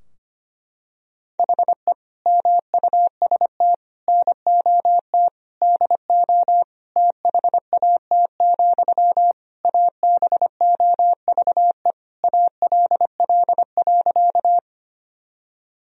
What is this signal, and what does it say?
Morse code, heard as a keyed tone, about 25 wpm, 700 Hz. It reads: HE MUST NOT DO THAT, ABOVE ALL.
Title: Sentences from Top 1000 Words - Review Part 2 - 25wpm